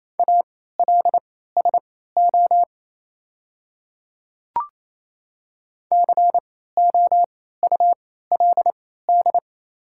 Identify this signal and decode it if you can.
Morse code, heard as a keyed tone, about 28 wpm, spaced as for 16 wpm, 700 Hz. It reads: ALSO COULD
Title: 3x Character Spacing - Top 200 Words - 28wpm